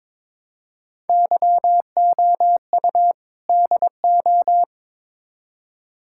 Morse code, heard as a keyed tone, about 22 wpm, 700 Hz. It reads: YOU DO
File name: Build A Phrase - 2-Word Phrases - 22wpm